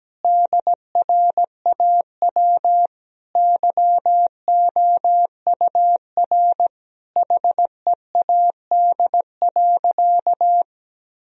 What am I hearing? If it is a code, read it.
Morse code, heard as a keyed tone, about 17 wpm, 700 Hz. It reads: DRAW YOUR HEAD.
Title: Sentences from Top 1000 Words - Review Part 1 - 17wpm